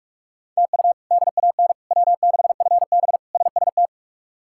Morse code, heard as a keyed tone, about 47 wpm, 700 Hz. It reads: TU DAN W6FB SST
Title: K1USN SST QSO Practice - 45wpm